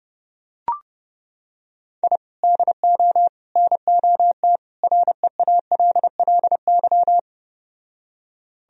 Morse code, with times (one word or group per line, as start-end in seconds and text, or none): 2.0-2.2 I
2.4-3.3 DO
3.6-4.6 NOT
4.8-7.2 REALLY